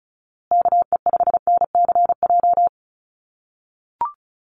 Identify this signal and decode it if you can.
Morse code, heard as a keyed tone, about 34 wpm, 700 Hz. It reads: KE5NCJ
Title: USA Call Signs Repeated 3x 35wpm